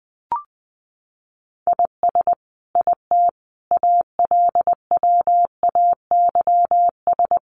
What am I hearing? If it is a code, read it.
Morse code, heard as a keyed tone, about 20 wpm, 700 Hz. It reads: IS IT ALWAYS
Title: Sentences from Top 1500 Words 20wpm